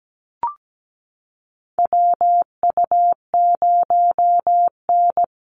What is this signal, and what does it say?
Morse code, heard as a keyed tone, about 17 wpm, 700 Hz. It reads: WU0N
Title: USA Call Signs Repeated 2x 17wpm